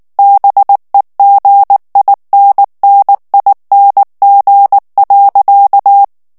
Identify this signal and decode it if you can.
Morse code, heard as a keyed tone, about 19 wpm, 800 Hz. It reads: BEGINNING.
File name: morse mp3 / ANGLAIS texte clair 2 vitesse 18 mots minute